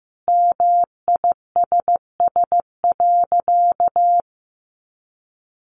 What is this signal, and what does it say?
Morse code, heard as a keyed tone, about 15 wpm, 700 Hz. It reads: MISS.
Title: Sentences from Top 700 Words 15wpm